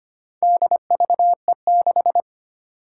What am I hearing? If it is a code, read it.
Morse code, heard as a keyed tone, about 25 wpm, 700 Hz. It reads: DVE6
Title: Four Challenging Letter-Number Combinations 25wpm